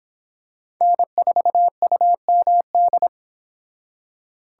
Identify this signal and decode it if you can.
Morse code, heard as a keyed tone, about 26 wpm, 700 Hz. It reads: N4UMD